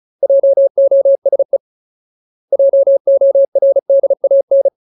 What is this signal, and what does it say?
Morse code, heard as a keyed tone, about 34 wpm, 550 Hz. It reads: JOSE JORDAN
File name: Ultimate Word List / 35WPM Ultimate